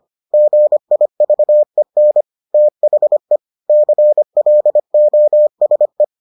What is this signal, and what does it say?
Morse code, heard as a keyed tone, about 25 wpm, 600 Hz. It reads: GIVEN THE CLOSE